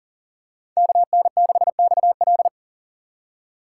Morse code, heard as a keyed tone, about 40 wpm, 700 Hz. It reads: KN6XL
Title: USA Call Signs - 4x Character Spacing - 40wpm